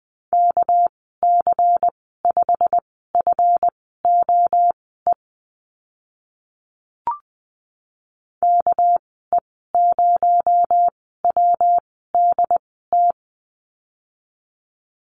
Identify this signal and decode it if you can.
Morse code, heard as a keyed tone, about 19 wpm, spaced as for 14 wpm, 700 Hz. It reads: KC5FOE KE0WDT
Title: 245 Lesson 39g - Letter z - Call Signs 20wpm